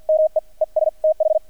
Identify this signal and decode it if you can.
Morse code, heard as a keyed tone, about 48 wpm, 630 Hz. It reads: ME ES TH
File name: archived / CRSnet